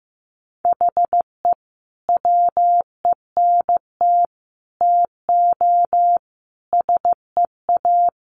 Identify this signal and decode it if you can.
Morse code, heard as a keyed tone, about 15 wpm, 700 Hz. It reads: HE WENT TO SEA
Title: Sentences from Top 1200 Words 15wpm